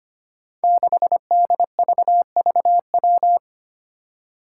Morse code, heard as a keyed tone, about 25 wpm, 700 Hz. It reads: 6DVVW
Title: Five Challenging Letter-Number Combinations 25wpm